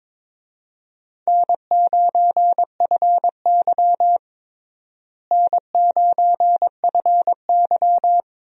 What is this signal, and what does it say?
Morse code, heard as a keyed tone, about 22 wpm, 700 Hz. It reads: N9FY N9FY
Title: USA Call Signs - 4 Characters - Repeated 2x - 22wpm